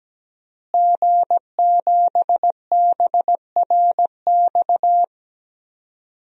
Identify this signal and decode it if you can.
Morse code, heard as a keyed tone, about 17 wpm, 700 Hz. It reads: G7BRX
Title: DXCC-100 Call Signs - 5 Characters - Repeated 2x - 17wpm